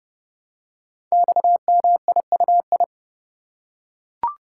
Morse code, heard as a keyed tone, about 30 wpm, 700 Hz. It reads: XMIUI
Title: Code Groups - 5 Letters - 30wpm